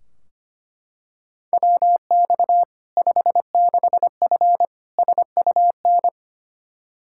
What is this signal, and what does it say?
Morse code, heard as a keyed tone, about 25 wpm, 700 Hz. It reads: WX 56F SUN
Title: Practice QSO - Exchange 2 - Info - 25wpm